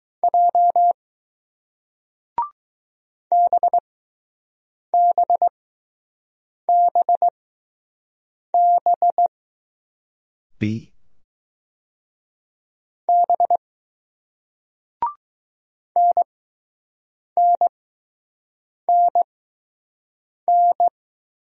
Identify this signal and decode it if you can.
Morse code, heard as a keyed tone, about 20 wpm, spaced as for 6 wpm, 700 Hz. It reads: J BBBB B NNNN